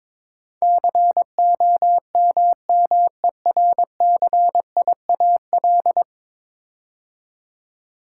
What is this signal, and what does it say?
Morse code, heard as a keyed tone, about 22 wpm, 700 Hz. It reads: COMMERCIAL
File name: practice / 22wpm / 10 Letter Words - First 3 Letters Given - 22wpm